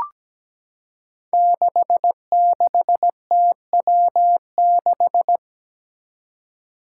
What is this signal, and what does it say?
Morse code, heard as a keyed tone, about 17 wpm, 700 Hz. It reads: 66TW6
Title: Five Challenging Letter-Number Combinations 17wpm